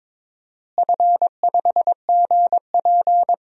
Morse code, heard as a keyed tone, about 23 wpm, 700 Hz. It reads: F5GP
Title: DXCC-100 Call Signs - 4 Characters - Repeated 2x - 22wpm